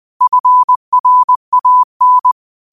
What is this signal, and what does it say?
Morse code, heard as a keyed tone, about 20 wpm, 1000 Hz. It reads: FRAN
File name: 20wpm